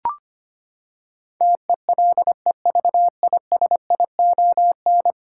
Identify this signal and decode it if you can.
Morse code, heard as a keyed tone, about 25 wpm, 700 Hz. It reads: TELEVISION
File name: Ten Letter Words - 25wpm